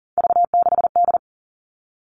Morse code, heard as a keyed tone, about 40 wpm, 700 Hz. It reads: V6D
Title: All Cut Numbers - 3-Digit 40wpm